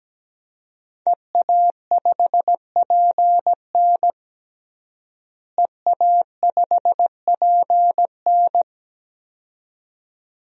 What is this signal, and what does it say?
Morse code, heard as a keyed tone, about 17 wpm, 700 Hz. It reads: EA5PN EA5PN
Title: DXCC-100 Call Signs - 5 Characters - Repeated 2x - 17wpm